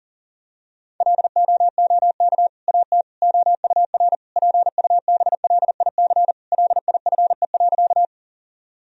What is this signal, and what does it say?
Morse code, heard as a keyed tone, about 40 wpm, 700 Hz. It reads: LOOK AT OUR PUBLIC LIFE.
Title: Sentences from Top 1000 Words - Review Part 3 - 40wpm